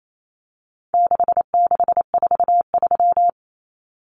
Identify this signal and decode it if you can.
Morse code, heard as a keyed tone, about 28 wpm, 700 Hz. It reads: 6643
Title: Common Cut Numbers - 4-Digit 28wpm